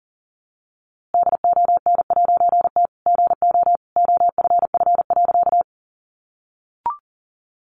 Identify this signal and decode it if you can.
Morse code, heard as a keyed tone, about 40 wpm, 700 Hz. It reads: DON'T GO OFF.
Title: A Sound Of Thunder 40wpm